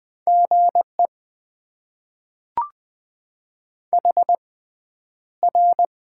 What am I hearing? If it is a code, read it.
Morse code, heard as a keyed tone, about 20 wpm, spaced as for 10 wpm, 700 Hz. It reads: GE HR
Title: Top 90 QSO Elements - 6x Character Spacing - 20wpm